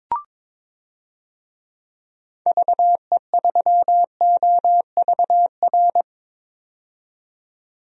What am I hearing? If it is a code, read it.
Morse code, heard as a keyed tone, about 22 wpm, 700 Hz. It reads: VE3OVR
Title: DXCC-100 Call Signs - 6 Characters - 22wpm